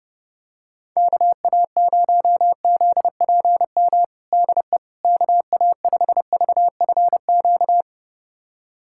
Morse code, heard as a keyed tone, about 32 wpm, 700 Hz. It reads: KA0ZPM DE KA5VFQ
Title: USA Callsign DE Callsign - 6 Characters - 30wpm